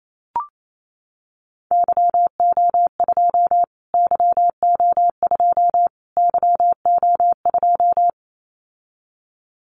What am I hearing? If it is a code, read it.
Morse code, heard as a keyed tone, about 28 wpm, 700 Hz. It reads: YO2 YO2 YO2
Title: Callsign Prefixes for Top 25 DXCC - Repeated 3x - 28wpm